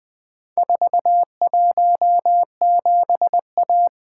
Morse code, heard as a keyed tone, about 20 wpm, 700 Hz. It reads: 417A
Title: Four Letter-Number Combinations - 20wpm